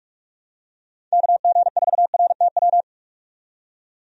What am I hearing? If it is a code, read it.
Morse code, heard as a keyed tone, about 46 wpm, 700 Hz. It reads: KG4RTW
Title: USA Call Signs - 6 Characters - 45wpm